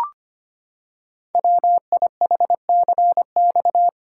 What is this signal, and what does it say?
Morse code, heard as a keyed tone, about 25 wpm, 700 Hz. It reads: WIHCX
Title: Code Groups - 5 Letters - 25wpm